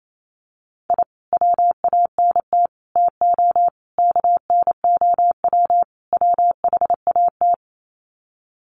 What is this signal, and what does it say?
Morse code, heard as a keyed tone, about 28 wpm, 700 Hz. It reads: I WANT TO KNOW WHAT
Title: Sets of 5 Words Encore2 28wpm